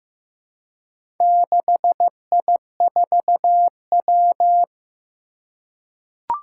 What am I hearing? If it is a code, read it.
Morse code, heard as a keyed tone, about 15 wpm, 700 Hz. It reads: 6I4W
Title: Four Challenging Letter-Number Combinations 15wpm